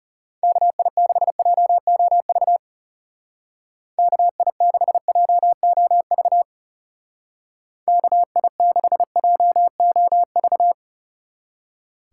Morse code, heard as a keyed tone, about 33 wpm, 700 Hz. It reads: KI6JOV KI6JOV KI6JOV
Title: Speed Racing - USA Call Signs 30wpm